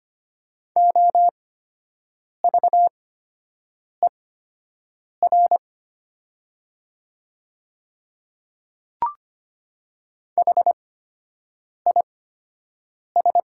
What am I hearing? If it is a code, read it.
Morse code, heard as a keyed tone, about 25 wpm, spaced as for 7 wpm, 700 Hz. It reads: OVER HIS
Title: 8x Character Spacing - Top 200 Words - 25wpm